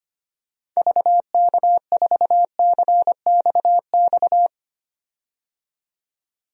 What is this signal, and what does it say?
Morse code, heard as a keyed tone, about 25 wpm, 700 Hz. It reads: VK4CXX
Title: DXCC-100 Call Signs - 6 Characters - 25wpm